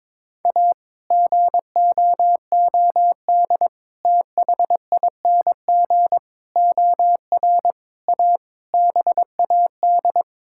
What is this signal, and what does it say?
Morse code, heard as a keyed tone, about 22 wpm, 700 Hz. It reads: A GOOD THING OR A BAD
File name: practice / 22wpm / Sets of 7 Words - 22wpm